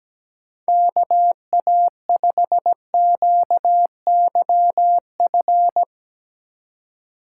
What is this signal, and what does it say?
Morse code, heard as a keyed tone, about 17 wpm, 700 Hz. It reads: KA5QYF
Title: USA Call Signs - 6 Characters - 4x Character Spacing - 17wpm